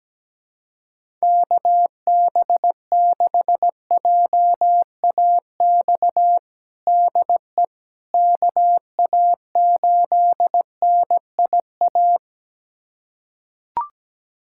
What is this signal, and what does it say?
Morse code, heard as a keyed tone, about 17 wpm, 700 Hz. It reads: KB6JAX DE KA8NIA
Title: USA Callsign DE Callsign - 6 Characters - 17wpm